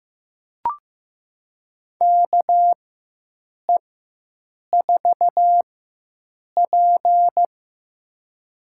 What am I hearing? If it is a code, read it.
Morse code, heard as a keyed tone, about 15 wpm, spaced as for 7 wpm, 700 Hz. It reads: KE4P
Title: USA Call Signs - 4 Characters - 4x Character Spacing - 15wpm